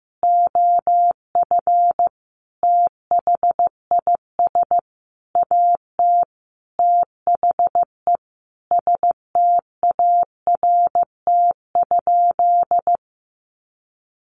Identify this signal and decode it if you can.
Morse code, heard as a keyed tone, about 15 wpm, 700 Hz. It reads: OF THIS AT THE START?
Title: Sentences from Top 1500 Words 15wpm